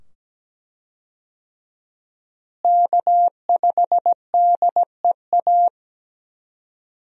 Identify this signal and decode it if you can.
Morse code, heard as a keyed tone, about 17 wpm, 700 Hz. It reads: K5DEA
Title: USA Call Signs - 5 Characters - Repeated 2x - 17wpm